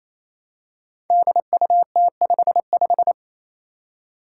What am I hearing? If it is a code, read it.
Morse code, heard as a keyed tone, about 27 wpm, 700 Hz. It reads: DUT55
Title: Five Challenging Letter-Number Combinations 28wpm